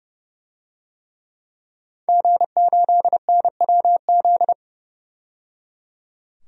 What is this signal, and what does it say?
Morse code, heard as a keyed tone, about 30 wpm, 700 Hz. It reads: G8NWZ
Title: DXCC-100 Call Signs - 5 Characters - 30wpm